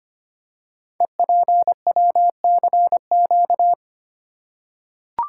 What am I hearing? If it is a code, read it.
Morse code, heard as a keyed tone, about 25 wpm, 700 Hz. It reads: EPWCQ
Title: Code Groups - 5 Letter:Numbers - 25wpm